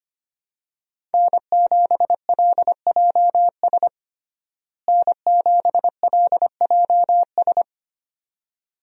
Morse code, heard as a keyed tone, about 25 wpm, 700 Hz. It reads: N7LJS N7LJS
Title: USA Call Signs - 5 Characters - Repeated 2x - 25wpm